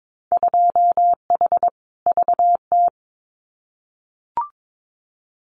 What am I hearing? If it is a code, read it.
Morse code, heard as a keyed tone, about 22 wpm, 700 Hz. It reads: 2H VT
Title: Winter Field Day Exchanges 22wpm